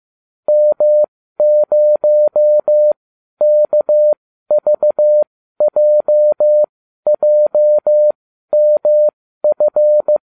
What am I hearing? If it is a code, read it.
Morse code, heard as a keyed tone, about 15 wpm, spaced as for 12 wpm, 600 Hz. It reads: M0KVJJMF
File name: cw / ubacw20 3